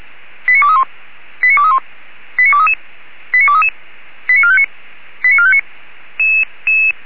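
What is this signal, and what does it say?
Five-tone selective call (ZVEI), heard as quick runs of five tones, tones of about 70 ms. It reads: 89231 89231 89233 89233 89459 89459
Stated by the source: ZVEI five-tone call received by SDR.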